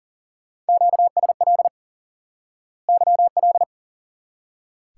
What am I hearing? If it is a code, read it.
Morse code, heard as a keyed tone, about 40 wpm, 700 Hz. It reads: QSL YL
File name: course / instant-qso-element-course-3 / IQE Course 3 - Lesson 89 - YAGI - Two Elements - 40wpm